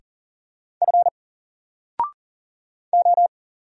Morse code, heard as a keyed tone, about 40 wpm, 700 Hz. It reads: F O
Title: Single Character (Letter:Number:Punctuation:Prosign) 40wpm